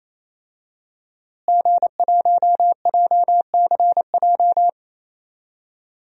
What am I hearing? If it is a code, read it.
Morse code, heard as a keyed tone, about 28 wpm, 700 Hz. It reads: G1JCJ
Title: DXCC-100 Call Signs - 5 Characters - Repeated 3x - 28wpm